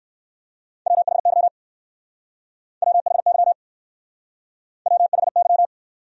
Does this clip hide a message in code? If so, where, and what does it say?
Yes: Morse code from 0.9 to 5.7 s, heard as a keyed tone, about 60 wpm, 700 Hz. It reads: WHY WHY WHY